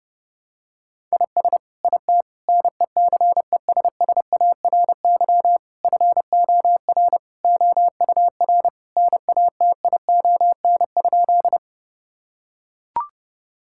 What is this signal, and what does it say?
Morse code, heard as a keyed tone, about 30 wpm, 700 Hz. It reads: IS IT NECESSARY FOR OUR NATION?
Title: Sentences from Top 1000 Words - Review Part 6 - 30wpm